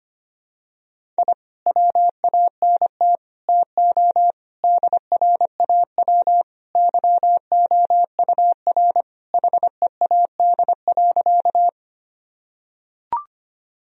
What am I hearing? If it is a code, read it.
Morse code, heard as a keyed tone, about 25 wpm, 700 Hz. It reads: I WANT TO DRAW YOUR HEAD.
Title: Sentences from Top 1000 Words - Review Part 1 - 25wpm